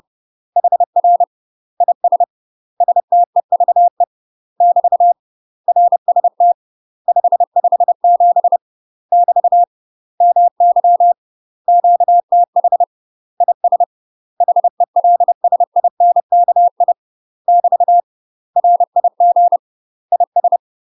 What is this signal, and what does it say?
Morse code, heard as a keyed tone, about 30 wpm, 700 Hz. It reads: HR IS STEVE <BT> RST 557 <BT> MY QTH IS HELSINKI <BT> RIG IS
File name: UK MASTER QSOs TRAINING / 30WPM / UK MASTER QSOs 0034